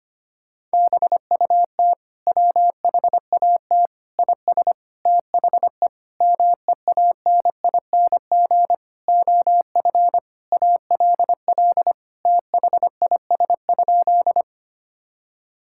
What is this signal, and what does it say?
Morse code, heard as a keyed tone, about 25 wpm, 700 Hz. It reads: BUT WHAT IS THE MEANING OF ALL THIS?
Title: Sentences from Top 1000 Words - Review Part 6 - 25wpm